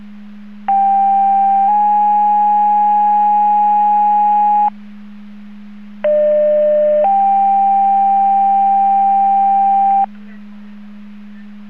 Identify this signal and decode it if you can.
Two-tone page, heard as a one-second tone, then a three-second tone, about 790 then 830 Hz, about 600 then 790 Hz. It is heard from 0.7 to 10.1 s.
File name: Portage 154.385 Paging